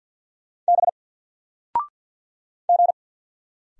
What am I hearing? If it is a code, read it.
Morse code, heard as a keyed tone, about 50 wpm, 700 Hz. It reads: B G
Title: Single Letter-Number 50wpm